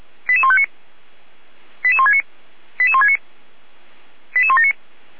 Five-tone selective call (ZVEI), heard as quick runs of five tones, tones of about 70 ms. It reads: 88159 88169 88169 88179